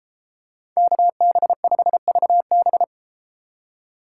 Morse code, heard as a keyed tone, about 33 wpm, 700 Hz. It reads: KB5VB